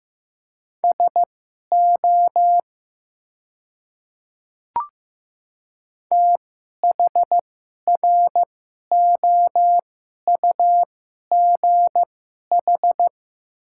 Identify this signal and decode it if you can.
Morse code, heard as a keyed tone, about 15 wpm, spaced as for 11 wpm, 700 Hz. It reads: SO THROUGH
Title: 2x Character Spacing - Top 200 Words - 15wpm